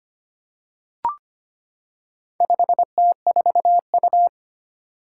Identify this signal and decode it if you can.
Morse code, heard as a keyed tone, about 25 wpm, 700 Hz. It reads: 5T4U
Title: Four Challenging Letter-Number Combinations 25wpm